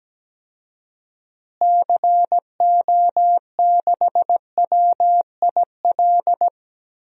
Morse code, heard as a keyed tone, about 17 wpm, 700 Hz. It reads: CO6WIL
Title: DXCC-100 Call Signs - 6 Characters - Repeated 2x - 17wpm